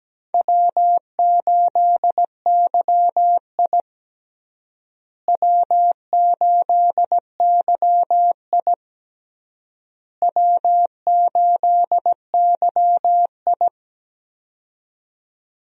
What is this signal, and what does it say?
Morse code, heard as a keyed tone, about 17 wpm, 700 Hz. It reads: W8YI W8YI W8YI
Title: USA Call Signs - 4 Characters - Repeated 3x - 17wpm